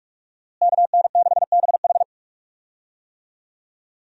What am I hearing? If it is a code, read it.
Morse code, heard as a keyed tone, about 47 wpm, 700 Hz. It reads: KN6BH
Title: USA Call Signs - 5 Characters - Repeated 2x - 45wpm